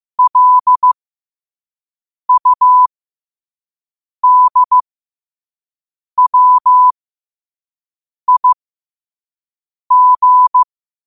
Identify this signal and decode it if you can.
Morse code, heard as a keyed tone, about 15 wpm, spaced as for 5 wpm, 1000 Hz. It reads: LUDWIG
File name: qotdfiles / 5wpm